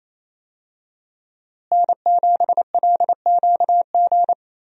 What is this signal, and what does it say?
Morse code, heard as a keyed tone, about 28 wpm, 700 Hz. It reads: N7LQG
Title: USA Call Signs - 5 Characters - Repeated 3x - 28wpm